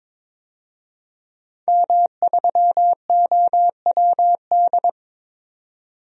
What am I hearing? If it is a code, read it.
Morse code, heard as a keyed tone, about 22 wpm, 700 Hz. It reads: M3OWD
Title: DXCC-100 Call Signs - 5 Characters - 22wpm